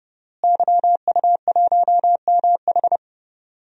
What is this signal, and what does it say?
Morse code, heard as a keyed tone, about 30 wpm, 700 Hz. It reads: YU1MH
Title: DXCC-100 Call Signs - 5 Characters - Repeated 3x - 30wpm